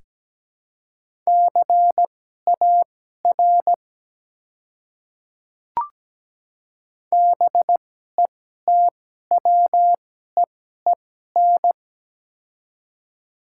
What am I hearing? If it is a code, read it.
Morse code, heard as a keyed tone, about 17 wpm, spaced as for 12 wpm, 700 Hz. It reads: CAR BETWEEN
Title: 2x Character Spacing - Top 300 Words - 17wpm